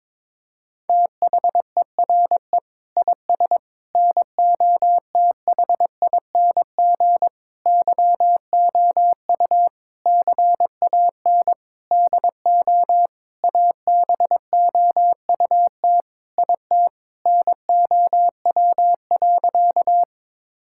Morse code, heard as a keyed tone, about 22 wpm, 700 Hz. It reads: THERE IS NOTHING YOU CAN DO ABOUT IT NOW.
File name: Sentences with 7-Word Phrases - 22wpm